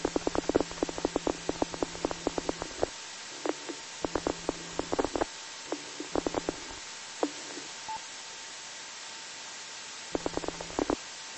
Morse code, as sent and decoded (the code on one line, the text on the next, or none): none